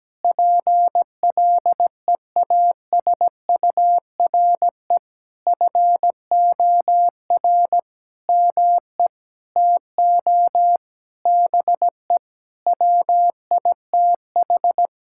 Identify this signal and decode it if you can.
Morse code, heard as a keyed tone, about 17 wpm, 700 Hz. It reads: PLEASURE FOR ME TO BE WITH
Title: Sentences from Top 1600 Words 17wpm